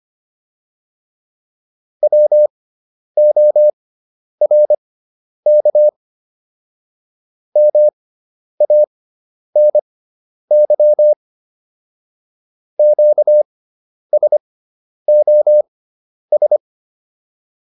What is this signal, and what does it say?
Morse code, heard as a keyed tone, about 25 wpm, spaced as for 10 wpm, 600 Hz. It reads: WORK MANY QSOS